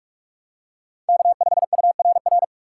Morse code, heard as a keyed tone, about 45 wpm, 690 Hz. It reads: K5URR
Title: USA Call Signs - 5 Characters - 2x Character Spacing - 45wpm